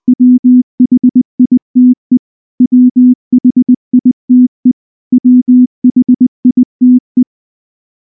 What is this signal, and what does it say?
Morse code, heard as a keyed tone, about 20 wpm, 250 Hz. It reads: WHITE WHITE WHITE
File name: morse / morse-training-20WPM-2